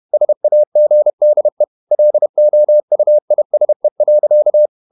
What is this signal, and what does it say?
Morse code, heard as a keyed tone, about 31 wpm, 600 Hz. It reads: SAGDE LOUISE.
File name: Ferie QRQ 31wpm